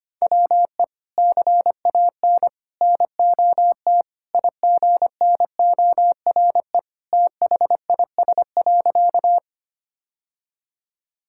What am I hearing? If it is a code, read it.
Morse code, heard as a keyed tone, about 25 wpm, 700 Hz. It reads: WE CAN NOT IGNORE THIS.
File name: practice / 25wpm / Sentences with 3-Word Phrases - 25wpm